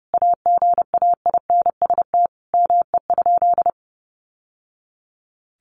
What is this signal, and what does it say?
Morse code, heard as a keyed tone, about 30 wpm, 700 Hz. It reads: AGAINST ME?
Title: Sentences from Top 600 Words 30wpm